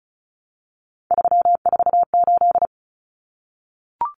Morse code, heard as a keyed tone, about 34 wpm, 700 Hz. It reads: 348
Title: Common Cut Numbers - 3-Digit 35wpm